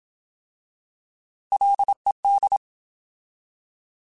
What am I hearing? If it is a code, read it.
Morse code, heard as a keyed tone, about 27 wpm, 800 Hz. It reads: LED